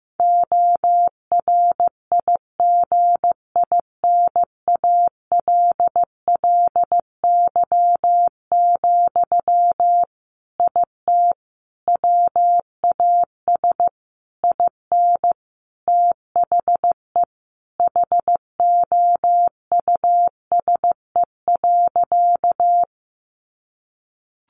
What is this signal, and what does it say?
Morse code, heard as a keyed tone, about 14 wpm, 700 Hz. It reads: ORIGINALLY, IT WAS IN THE HOUSE.
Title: Sentences from Top 1500 Words 15wpm